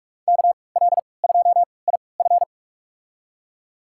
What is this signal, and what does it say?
Morse code, heard as a keyed tone, about 46 wpm, spaced as for 25 wpm, 700 Hz. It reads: KL2IF
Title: USA Call Signs - 5 Characters - 3x Character Spacing - 45wpm